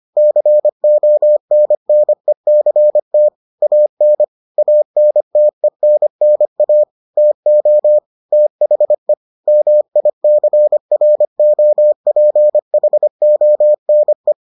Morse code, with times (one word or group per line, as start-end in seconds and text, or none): 0.2-3.3 CONNECT
3.6-4.2 AN
4.6-6.8 ANTENNA
7.2-8.0 TO
8.3-9.1 THE
9.5-14.3 MICROPHONE